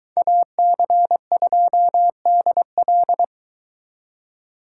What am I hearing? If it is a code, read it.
Morse code, heard as a keyed tone, about 23 wpm, 700 Hz. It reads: AC2DL